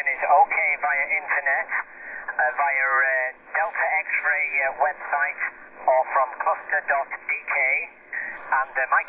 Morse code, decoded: none